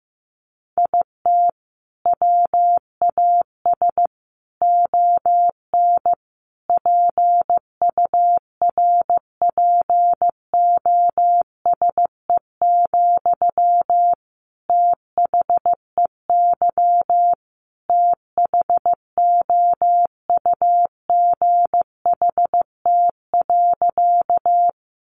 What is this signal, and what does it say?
Morse code, heard as a keyed tone, about 15 wpm, 700 Hz. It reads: IT WAS ON PURPOSE, THEY THOUGHT.
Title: Sentences from Top 800 Words 15wpm